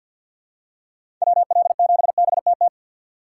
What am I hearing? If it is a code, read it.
Morse code, heard as a keyed tone, about 46 wpm, 700 Hz. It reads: WL7BTT
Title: USA Call Signs - 6 Characters - 3x Character Spacing - 50wpm